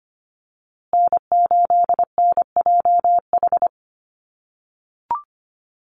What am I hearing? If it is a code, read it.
Morse code, heard as a keyed tone, about 25 wpm, 700 Hz. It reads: N8NJH
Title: USA Call Signs Repeated 3x 25wpm